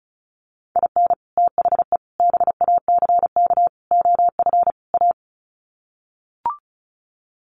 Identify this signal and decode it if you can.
Morse code, heard as a keyed tone, about 34 wpm, 700 Hz. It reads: IN THE BACK OF A
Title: Sets of 5 Words Encore2 35wpm